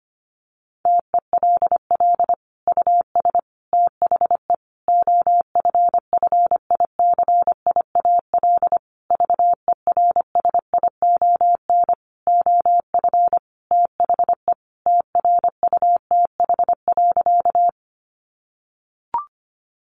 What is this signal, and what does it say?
Morse code, heard as a keyed tone, about 25 wpm, 700 Hz. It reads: TELL US THE OFFICIAL VERSION OF THE TRUTH.
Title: Sentences from Top 1400 Words 25wpm